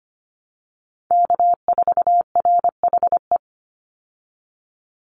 Morse code, heard as a keyed tone, about 25 wpm, 700 Hz. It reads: K4RHE
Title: USA Call Signs 25wpm